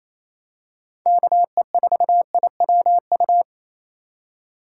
Morse code, heard as a keyed tone, about 28 wpm, 700 Hz. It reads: KE4IWU
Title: USA Call Signs - 6x Character Spacing - 28wpm